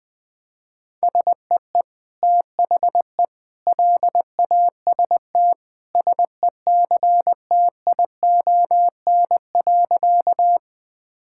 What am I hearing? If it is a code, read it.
Morse code, heard as a keyed tone, about 21 wpm, 700 Hz. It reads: SEE THE LAST SECTION.
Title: Sentences from Top 1000 Words - Review Part 5 - 20wpm